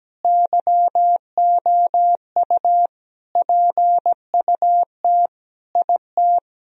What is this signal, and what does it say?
Morse code, heard as a keyed tone, about 17 wpm, 700 Hz. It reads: YOU PUT IT
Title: Sentences from Top 1000 Words - Review Part 4 - 17wpm